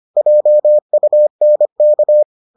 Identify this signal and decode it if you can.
Morse code, heard as a keyed tone, about 25 wpm, 600 Hz. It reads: JUNK